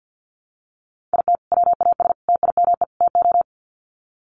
Morse code, heard as a keyed tone, about 50 wpm, 700 Hz. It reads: IT WAS TIME TO
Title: Sets of 4 Words 50wpm